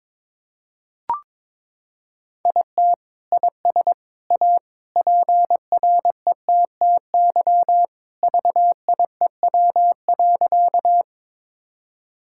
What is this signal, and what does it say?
Morse code, heard as a keyed tone, about 22 wpm, 700 Hz. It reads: IT IS A PRETTY VIEW.
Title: Sentences from Top 1000 Words - Review Part 6 - 22wpm